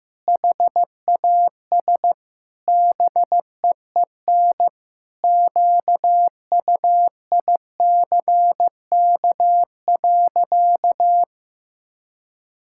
Morse code, heard as a keyed tone, about 15 wpm, 700 Hz. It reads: HAS BEEN QUICK.
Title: Sentences from Top 1000 Words - Review Part 5 - 15wpm